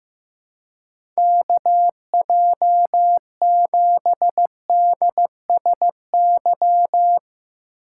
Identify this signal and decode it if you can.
Morse code, heard as a keyed tone, about 15 wpm, 700 Hz. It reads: KJ7DSY
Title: USA Call Signs - 6 Characters - 2x Character Spacing - 15wpm